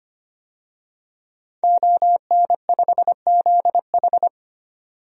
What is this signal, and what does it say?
Morse code, heard as a keyed tone, about 25 wpm, 700 Hz. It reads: ON5ZH
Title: DXCC-100 Call Signs - 5 Characters - 25wpm